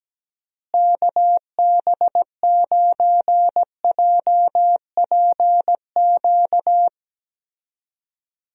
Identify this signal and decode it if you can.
Morse code, heard as a keyed tone, about 17 wpm, 700 Hz. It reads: KB9JPQ
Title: USA Call Signs - 6 Characters - Repeated 3x - 17wpm